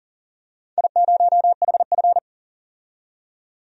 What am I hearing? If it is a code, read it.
Morse code, heard as a keyed tone, about 40 wpm, 700 Hz. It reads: I0HF
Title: DXCC-100 Call Signs - 4 Characters - 40wpm